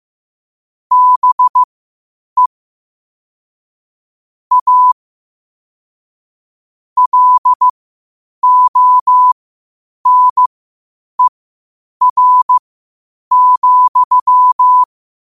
Morse code, heard as a keyed tone, about 15 wpm, spaced as for 8 wpm, 1000 Hz. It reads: BE A LONER,
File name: qotdfiles / 8wpm